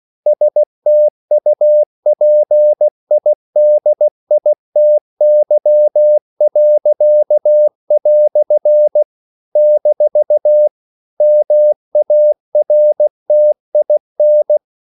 Morse code, heard as a keyed tone, about 16 wpm, 600 Hz. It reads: STUPIDITY." - MARTIN
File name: CW / Practice / EnglishQuotations0116